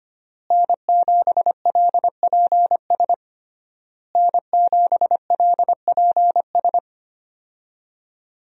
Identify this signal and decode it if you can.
Morse code, heard as a keyed tone, about 25 wpm, 700 Hz. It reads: N7LPS N7LPS
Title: USA Call Signs - 5 Characters - Repeated 3x - 25wpm